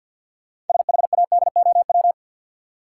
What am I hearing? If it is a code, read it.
Morse code, heard as a keyed tone, about 52 wpm, 700 Hz. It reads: SHADOW
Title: Sets of 5 Words Encore - New Word Drill - 50wpm